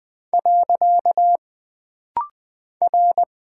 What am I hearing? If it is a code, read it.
Morse code, heard as a keyed tone, about 20 wpm, 700 Hz. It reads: . R
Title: Single Character (Letter:Number:Punctuation:Prosign) - Rapid-Fire - 20wpm